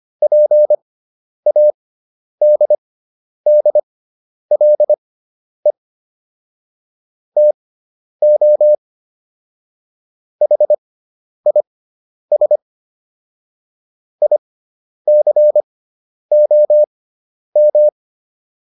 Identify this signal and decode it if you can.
Morse code, heard as a keyed tone, about 25 wpm, spaced as for 10 wpm, 600 Hz. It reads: PADDLE TO HIS ICOM